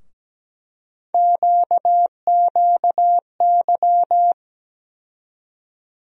Morse code, heard as a keyed tone, about 17 wpm, 700 Hz. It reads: QQY
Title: USA Call Sign Postfixes - 3 Letters 17wpm